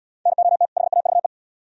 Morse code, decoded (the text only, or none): NOT HAVE